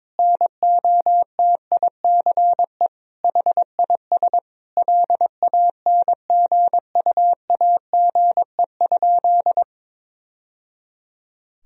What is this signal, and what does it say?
Morse code, heard as a keyed tone, about 22 wpm, 700 Hz. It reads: NOTICE HIS LANGUAGE?
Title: Sentences from Top 1000 Words - Review Part 3 - 22wpm